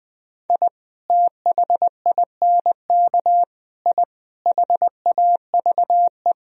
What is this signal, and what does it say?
Morse code, heard as a keyed tone, about 20 wpm, 700 Hz. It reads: I THINK I HAVE
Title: Sentences from Top 1000 Words - Review Part 3 - 20wpm